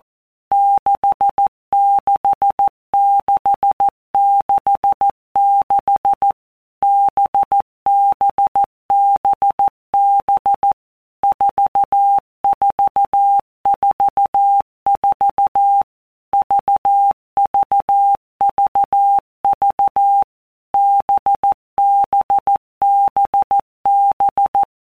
Morse code, heard as a keyed tone, about 14 wpm, 800 Hz. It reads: 66666 BBBB 4444 VVVV BBBB
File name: Basic Letter Groups By Blocks / Groups@13 / VB46@13